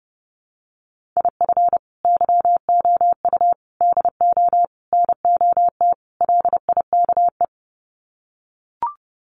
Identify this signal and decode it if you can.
Morse code, heard as a keyed tone, about 28 wpm, 700 Hz. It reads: IF YOU DO NOT LIKE
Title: Sets of 5 Words 30wpm